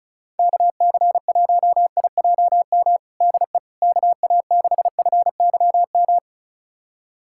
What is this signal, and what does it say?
Morse code, heard as a keyed tone, about 36 wpm, 700 Hz. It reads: KC1IJM DE KA6FYM
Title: USA Callsign DE Callsign - 6 Characters - 35wpm